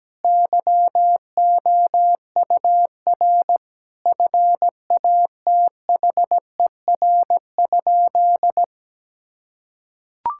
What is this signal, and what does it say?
Morse code, heard as a keyed tone, about 17 wpm, 700 Hz. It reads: YOUR FATHER?
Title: Sentences from Top 1000 Words - Review Part 5 - 17wpm